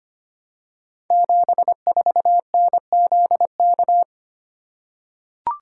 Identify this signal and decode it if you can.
Morse code, heard as a keyed tone, about 25 wpm, 700 Hz. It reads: 74NZK
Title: Code Groups - 5 Letter:Numbers - 25wpm